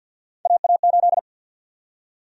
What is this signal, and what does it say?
Morse code, heard as a keyed tone, about 51 wpm, 700 Hz. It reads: AA8
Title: USA Call Sign Prefixes - 50wpm